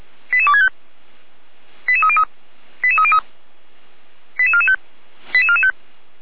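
Five-tone selective call (ZVEI), heard as quick runs of five tones, tones of about 70 ms. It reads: 88265 88303 88303 88445 88445
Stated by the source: ZVEI five-tone selective call received on an SDR.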